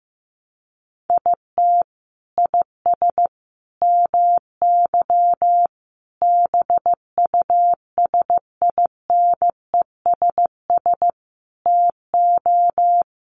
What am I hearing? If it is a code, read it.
Morse code, heard as a keyed tone, about 15 wpm, 700 Hz. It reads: IT IS MY BUSINESS TO
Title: Sentences from Top 600 Words 15wpm